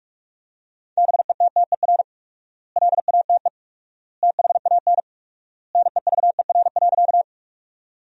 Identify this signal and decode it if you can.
Morse code, heard as a keyed tone, about 46 wpm, 700 Hz. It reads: BETTER LATE THAN NEVER.
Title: 4x Word-Spacing - Sentences from Top 400 Words - 45wpm